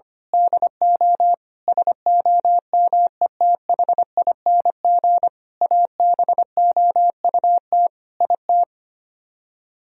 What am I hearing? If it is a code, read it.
Morse code, heard as a keyed tone, about 24 wpm, 700 Hz. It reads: DO SOMETHING ABOUT IT
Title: Speed-Racing - Sets of 5 Words 25wpm